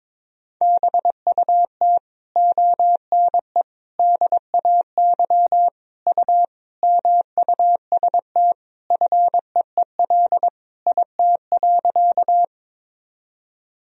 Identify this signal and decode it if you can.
Morse code, heard as a keyed tone, about 22 wpm, 700 Hz. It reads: BUT ONE DAY U MUST FEEL IT.
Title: Sentences from Top 400 Words and Abbreviations 22wpm